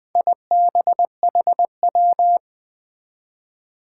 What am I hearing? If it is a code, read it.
Morse code, heard as a keyed tone, about 20 wpm, 700 Hz. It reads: IBHW
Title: Four Challenging Letter-Number Combinations 20wpm